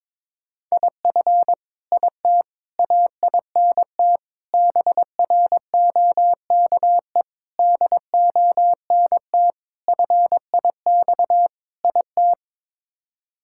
Morse code, heard as a keyed tone, about 22 wpm, 700 Hz. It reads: IF IT AINT BROKE DONT FIX IT
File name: practice / 22wpm / Sets of 7 Words - 22wpm